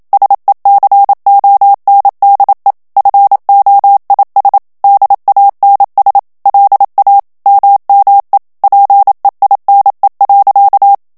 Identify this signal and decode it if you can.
Morse code, heard as a keyed tone, about 27 wpm, 800 Hz. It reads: SECONDE FOIS DANS LA MME PEINE.
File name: FRANCAIS TEXTE CLAIR 2 VITESSE 26 MOTS MINUTES